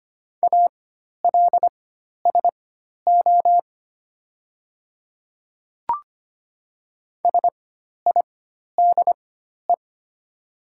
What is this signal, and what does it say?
Morse code, heard as a keyed tone, about 25 wpm, spaced as for 12 wpm, 690 Hz. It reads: ALSO SIDE
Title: 4x Character Spacing - Top 200 Words - 25wpm